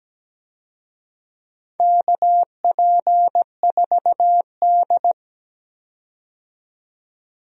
Morse code, heard as a keyed tone, about 17 wpm, 700 Hz. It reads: KP4D
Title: DXCC-100 Call Signs - 4 Characters - 17wpm